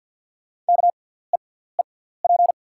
Morse code, heard as a keyed tone, about 48 wpm, spaced as for 17 wpm, 690 Hz. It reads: KEEP
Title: 6x Character Spacing - Top 500 Words - 50wpm